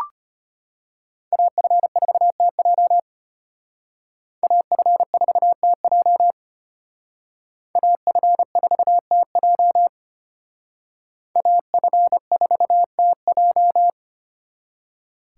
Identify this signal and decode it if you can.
Morse code, heard as a keyed tone, about 33 wpm, 700 Hz. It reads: AF4TJ AF4TJ AF4TJ AF4TJ